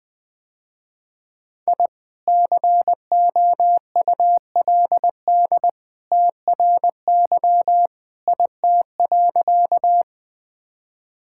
Morse code, heard as a keyed tone, about 20 wpm, 700 Hz. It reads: I COULD TRY IT.
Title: Sentences with 2-Word Phrases - Spoken Beginning - 20wpm